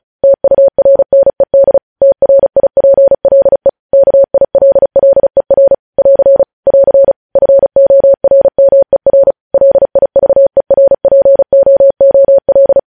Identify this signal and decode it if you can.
Morse code, heard as a keyed tone, about 36 wpm, 550 Hz. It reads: TURNED TRIPLE KILLER <AR> <AR> FORMER LIVERPOOL
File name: podcast / News headlines-1x-35x35